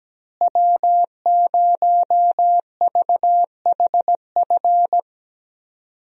Morse code, heard as a keyed tone, about 17 wpm, 700 Hz. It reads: W0VHF
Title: USA Call Signs - 5 Characters - Repeated 3x - 17wpm